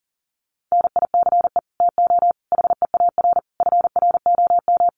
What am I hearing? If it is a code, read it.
Morse code, heard as a keyed tone, about 40 wpm, 700 Hz. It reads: NICE TO HEAR FROM
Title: Common QSO Phrases 40wpm